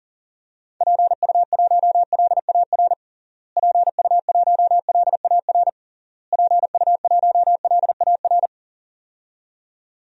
Morse code, heard as a keyed tone, about 40 wpm, 700 Hz. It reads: PU1LAR PU1LAR PU1LAR
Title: DXCC-100 Call Signs - 6 Characters - Repeated 3x - 40wpm